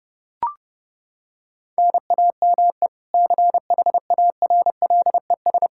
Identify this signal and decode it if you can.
Morse code, heard as a keyed tone, about 30 wpm, 700 Hz. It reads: NAME CHARLES
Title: Practice QSO - Exchange 1 - Beginning - 30wpm